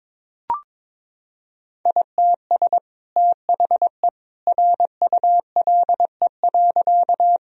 Morse code, heard as a keyed tone, about 22 wpm, 700 Hz. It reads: ITS THE RULE.
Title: Sentences from Top 1000 Words - Review Part 2 - 22wpm